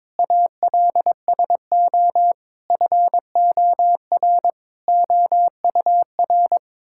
Morse code, heard as a keyed tone, about 22 wpm, 700 Hz. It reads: ALSO FOR OUR
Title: Sentences from Top 1000 Words - Review Part 6 - 22wpm